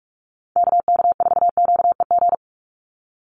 Morse code, heard as a keyed tone, about 44 wpm, 700 Hz. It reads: KK4QEG
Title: USA Call Signs 45wpm